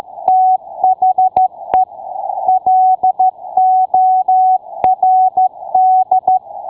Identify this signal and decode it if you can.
Morse code, heard as a keyed tone, about 13 wpm, 750 Hz. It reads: THE LORD